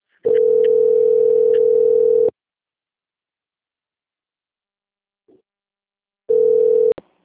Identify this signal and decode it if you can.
Ringback tone (North America), 440 + 480 Hz.
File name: out- - - - - .61575